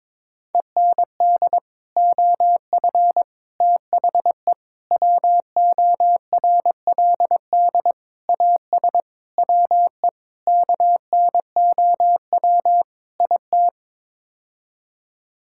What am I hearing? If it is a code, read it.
Morse code, heard as a keyed tone, about 22 wpm, 700 Hz. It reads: END OF THE WORLD AS WE KNOW IT
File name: practice / 22wpm / Sets of 8 Words - 22wpm